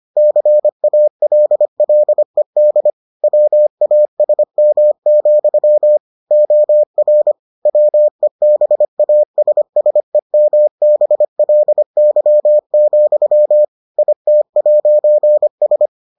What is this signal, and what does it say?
Morse code, heard as a keyed tone, about 25 wpm, 600 Hz. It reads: CALLED WASM, OR WEBASSEMBLY, IT'S